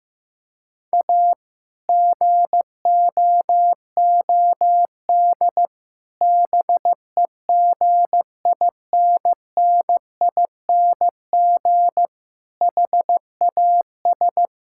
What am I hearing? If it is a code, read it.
Morse code, heard as a keyed tone, about 16 wpm, 700 Hz. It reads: A GOOD BEGINNING HAS
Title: Sentences from Top 1000 Words - Review Part 5 - 15wpm